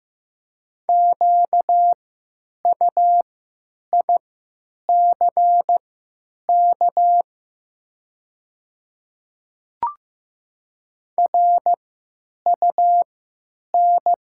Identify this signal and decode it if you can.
Morse code, heard as a keyed tone, about 15 wpm, spaced as for 9 wpm, 700 Hz. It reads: QUICK RUN
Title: 3x Character Spacing - Top 500 Words - 15wpm